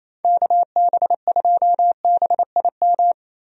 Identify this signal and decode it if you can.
Morse code, heard as a keyed tone, about 27 wpm, 700 Hz. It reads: KB2BIM
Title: USA Call Signs - 6 Characters - Repeated 2x - 28wpm